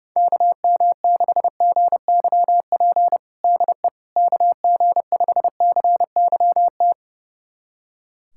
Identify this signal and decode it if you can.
Morse code, heard as a keyed tone, about 30 wpm, 700 Hz. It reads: KM6GYP DE KG5CYT
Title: USA Callsign DE Callsign - 6 Characters - 30wpm